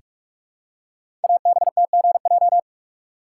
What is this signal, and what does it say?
Morse code, heard as a keyed tone, about 44 wpm, 700 Hz. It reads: ABTGJ
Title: Code Groups - 5 Letters - 45wpm